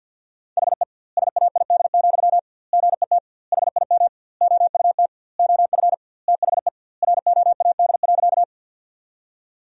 Morse code, heard as a keyed tone, about 50 wpm, 700 Hz. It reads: HE SAID, GET HIM OUT OF THE ROAD.
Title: 2x Word-Spacing - Sentences from Top 400 Words - 50wpm